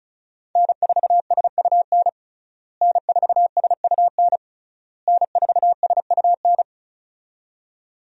Morse code, heard as a keyed tone, about 36 wpm, 700 Hz. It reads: N4SUN N4SUN N4SUN
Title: USA Call Signs - 5 Characters - Repeated 3x - 35wpm